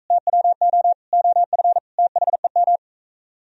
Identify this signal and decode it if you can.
Morse code, heard as a keyed tone, about 44 wpm, 700 Hz. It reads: TWO OF THEM